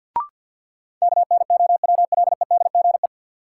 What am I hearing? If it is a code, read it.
Morse code, heard as a keyed tone, about 50 wpm, 700 Hz. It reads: KNOWLEDGE
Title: Sentences from Top 600 Words - New Word Drill - 50wpm